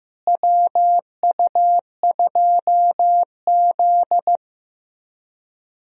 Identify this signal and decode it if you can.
Morse code, heard as a keyed tone, about 15 wpm, 700 Hz. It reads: WU2Z
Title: USA Call Signs - 4 Characters - Repeated 3x - 15wpm